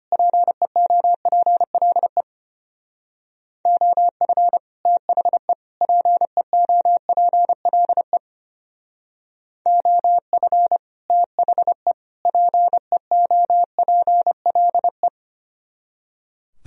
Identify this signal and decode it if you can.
Morse code, heard as a keyed tone, about 32 wpm, 700 Hz. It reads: PEOPLE OF THE PEOPLE OF THE PEOPLE